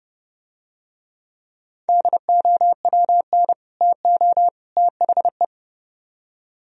Morse code, heard as a keyed tone, about 29 wpm, 700 Hz. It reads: DOWN TO THE